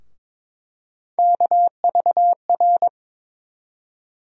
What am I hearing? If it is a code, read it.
Morse code, heard as a keyed tone, about 22 wpm, 700 Hz. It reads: KVR
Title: USA Call Sign Postfixes - 3 Letters 22wpm